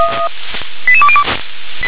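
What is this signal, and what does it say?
Five-tone selective call (ZVEI), heard as quick runs of five tones, tones of about 70 ms. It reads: 88202
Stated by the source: ZVEI five-tone selective call received by SDR.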